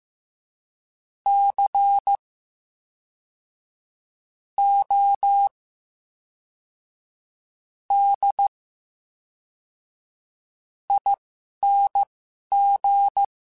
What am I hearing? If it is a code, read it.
Morse code, heard as a keyed tone, about 15 wpm, spaced as for 11 wpm, 800 Hz. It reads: C O D ING
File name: Special INGs / Special INGs-mp3s / VerySlowToFaster Ing @10